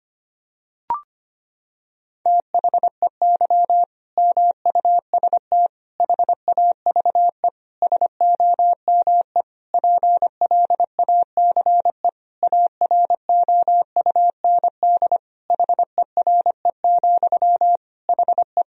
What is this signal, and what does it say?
Morse code, heard as a keyed tone, about 25 wpm, 700 Hz. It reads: THEY MUST HAVE SOME PLACE AROUND HERE, HE
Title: Sentences from Top 1000 Words - Review Part 3 - 25wpm